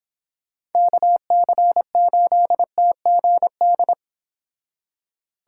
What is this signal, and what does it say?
Morse code, heard as a keyed tone, about 31 wpm, 700 Hz. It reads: KC8TGD